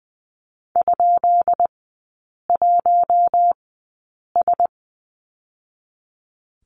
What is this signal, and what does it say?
Morse code, heard as a keyed tone, about 20 wpm, spaced as for 8 wpm, 690 Hz. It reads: ?1S
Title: 159 Lesson 26e - Letter g - Three Characters 20wpm